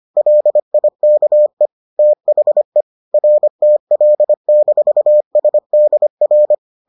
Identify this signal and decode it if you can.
Morse code, heard as a keyed tone, about 25 wpm, 600 Hz. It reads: LIKE THE RTL-SDR